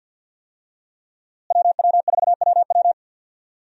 Morse code, heard as a keyed tone, about 46 wpm, 700 Hz. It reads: WW4WW
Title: USA Call Signs - 5 Characters - Repeated 2x - 50wpm